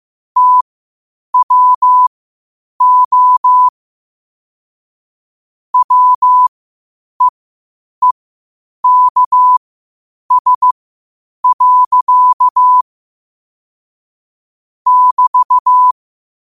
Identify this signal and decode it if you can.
Morse code, heard as a keyed tone, about 15 wpm, spaced as for 8 wpm, 1000 Hz. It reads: TWO WEEKS. <BT>
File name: qotdfiles / 8wpm